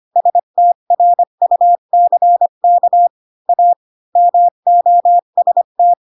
Morse code, heard as a keyed tone, about 25 wpm, 700 Hz. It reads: STRUCK A MOST
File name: Pinocchio / Pinocchio ch1 20 25 ik1hge